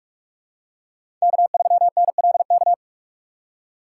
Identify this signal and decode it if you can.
Morse code, heard as a keyed tone, about 44 wpm, 700 Hz. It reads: K3NLK
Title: USA Call Signs - 5 Characters - 6x Character Spacing - 45wpm